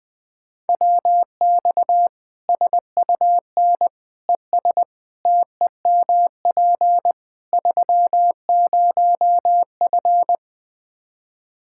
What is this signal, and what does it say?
Morse code, heard as a keyed tone, about 20 wpm, 700 Hz. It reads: WX SUN ES TEMP 30F
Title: Practice QSO - Exchange 2 - Info - 20wpm